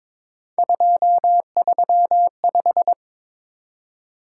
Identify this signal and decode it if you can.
Morse code, heard as a keyed tone, about 26 wpm, 700 Hz. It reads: 235
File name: Speed-Racing - Three Letter-Number Combinations - 22wpm